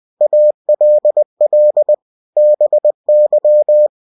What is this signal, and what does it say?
Morse code, heard as a keyed tone, about 20 wpm, 600 Hz. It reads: ALL BY